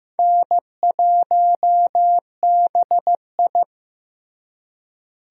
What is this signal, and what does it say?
Morse code, heard as a keyed tone, about 15 wpm, 700 Hz. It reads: N1BI
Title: USA Call Signs - 4 Characters - Repeated 2x - 15wpm